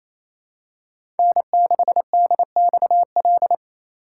Morse code, heard as a keyed tone, about 28 wpm, 700 Hz. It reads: N6DXL
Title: USA Call Signs - 5 Characters - 2x Character Spacing - 28wpm